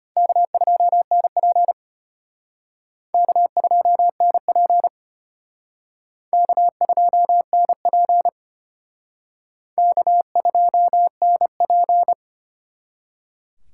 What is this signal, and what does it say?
Morse code, heard as a keyed tone, about 34 wpm, 700 Hz. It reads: K2NP K2NP K2NP K2NP